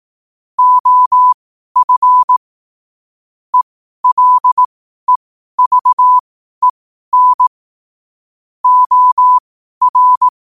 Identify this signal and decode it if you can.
Morse code, heard as a keyed tone, about 18 wpm, spaced as for 12 wpm, 1000 Hz. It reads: OF ELEVEN OR
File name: qotdfiles / 12wpm